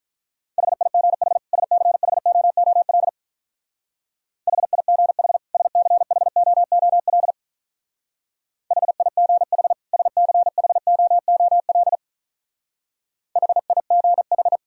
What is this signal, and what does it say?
Morse code, heard as a keyed tone, about 43 wpm, 700 Hz. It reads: HIGH SCHOOL HIGH SCHOOL HIGH SCHOOL HIGH
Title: Speed-Racing - Sets of 2 Words 35wpm